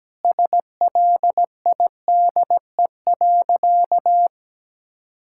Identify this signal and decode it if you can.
Morse code, heard as a keyed tone, about 17 wpm, 700 Hz. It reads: SLIDE.
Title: Sentences from Top 1600 Words 17wpm